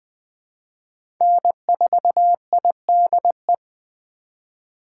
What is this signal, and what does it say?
Morse code, heard as a keyed tone, about 20 wpm, 700 Hz. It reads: N4IDE
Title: USA Call Signs - 5 Characters - 2x Character Spacing - 20wpm